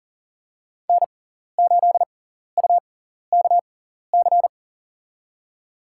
Morse code, heard as a keyed tone, about 38 wpm, spaced as for 14 wpm, 700 Hz. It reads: N8UKC
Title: USA Call Signs - 5 Characters - 6x Character Spacing - 40wpm